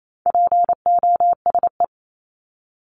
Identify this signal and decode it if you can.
Morse code, heard as a keyed tone, about 28 wpm, 700 Hz. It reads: POSE
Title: Top 500 Words 28wpm